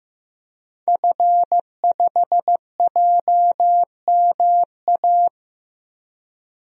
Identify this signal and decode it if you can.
Morse code, heard as a keyed tone, about 15 wpm, 700 Hz. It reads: F5JMA
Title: DXCC-100 Call Signs - 5 Characters - 15wpm